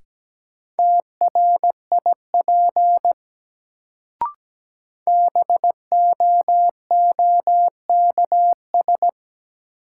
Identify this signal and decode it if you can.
Morse code, heard as a keyed tone, about 17 wpm, 700 Hz. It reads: TRIP BOOKS
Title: Sentences from Top 1000 Words - Review Part 4 - New Word Drill - 17wpm